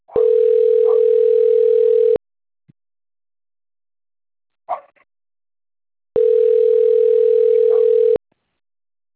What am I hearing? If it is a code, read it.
Ringback tone (North America), 440 + 480 Hz.